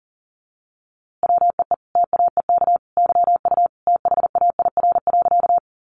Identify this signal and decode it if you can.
Morse code, heard as a keyed tone, about 40 wpm, 700 Hz. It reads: WEE TAEK YU THAIR.
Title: A Sound Of Thunder 40wpm